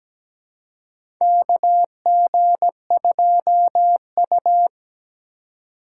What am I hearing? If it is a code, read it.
Morse code, heard as a keyed tone, about 17 wpm, 700 Hz. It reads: KG2U
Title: USA Call Signs - 4 Characters - 2x Character Spacing - 17wpm